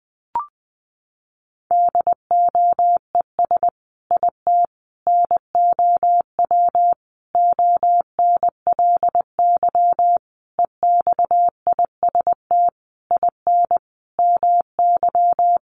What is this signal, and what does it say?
Morse code, heard as a keyed tone, about 20 wpm, 700 Hz. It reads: DOES IT NOW ONLY EXIST IN MY
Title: Sentences from Top 1300 Words 20wpm